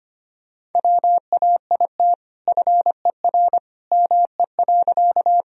Morse code, heard as a keyed tone, about 25 wpm, 700 Hz. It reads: WAIT FER ME.
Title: Sentences from Top 500 Words and Abbreviations 25wpm